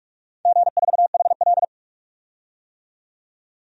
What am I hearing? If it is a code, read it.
Morse code, heard as a keyed tone, about 46 wpm, 700 Hz. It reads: G4HL
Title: DXCC-100 Call Signs - 4 Characters - 45wpm